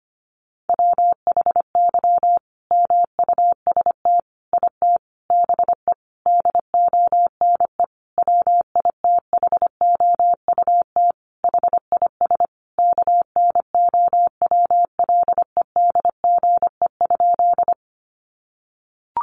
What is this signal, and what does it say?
Morse code, heard as a keyed tone, about 25 wpm, 700 Hz. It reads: WHY MUST IT BE DONE WITHOUT HIS KNOWLEDGE?
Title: Sentences from Top 600 Words 25wpm